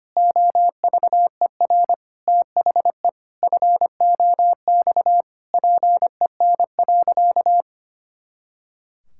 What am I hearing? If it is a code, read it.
Morse code, heard as a keyed tone, about 25 wpm, 700 Hz. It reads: OVER THE FOX PEN.
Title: Pangram Challenge v2 - 25wpm